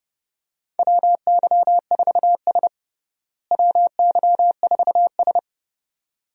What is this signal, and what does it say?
Morse code, heard as a keyed tone, about 31 wpm, 700 Hz. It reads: WY4S WY4S
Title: USA Call Signs - 4 Characters - Repeated 2x - 30wpm